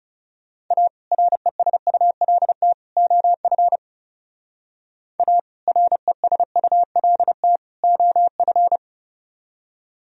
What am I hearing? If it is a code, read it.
Morse code, heard as a keyed tone, about 40 wpm, 700 Hz. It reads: A RESULT OF A RESULT OF